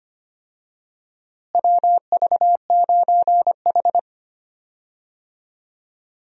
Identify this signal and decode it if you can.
Morse code, heard as a keyed tone, about 25 wpm, 700 Hz. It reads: WV9H
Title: USA Call Signs - 4 Characters - Repeated 3x - 25wpm